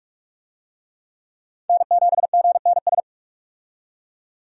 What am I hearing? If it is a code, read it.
Morse code, heard as a keyed tone, about 46 wpm, 700 Hz. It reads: N7GNS
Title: USA Call Signs - 5 Characters - 45wpm